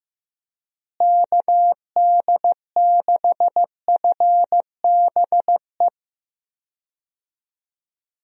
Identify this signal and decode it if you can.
Morse code, heard as a keyed tone, about 15 wpm, 700 Hz. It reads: KD6FBE
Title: USA Call Signs - 6 Characters - Repeated 2x - 15wpm